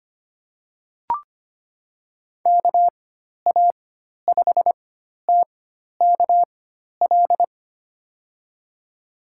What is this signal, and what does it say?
Morse code, heard as a keyed tone, about 25 wpm, spaced as for 12 wpm, 700 Hz. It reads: KA5TKL
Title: USA Call Signs - 6 Characters - 4x Character Spacing - 25wpm